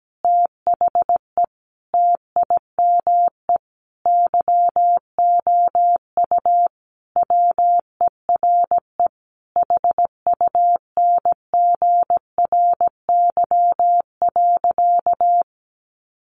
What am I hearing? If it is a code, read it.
Morse code, heard as a keyed tone, about 17 wpm, 700 Hz. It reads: THE TIME YOU WERE HUNGRY.
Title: Sentences from Top 1400 Words 17wpm